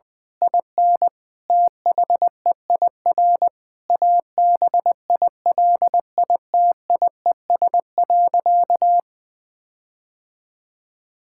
Morse code, heard as a keyed tone, about 20 wpm, 700 Hz. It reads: IN THEIR ABILITIES.
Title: Difficult Sentences with Hints - 20wpm